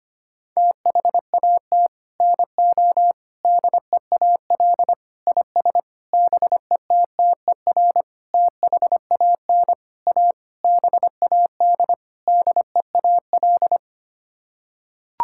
Morse code, heard as a keyed tone, about 25 wpm, 700 Hz. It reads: THAT NO DEAL IS BETTER THAN A BAD DEAL
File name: Sets of 9 Words - 25wpm